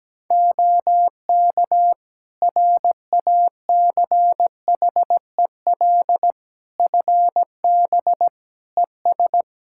Morse code, heard as a keyed tone, about 17 wpm, 700 Hz. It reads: OK RACHEL FB ES
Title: Practice QSO - Exchange 2 - Info - 17wpm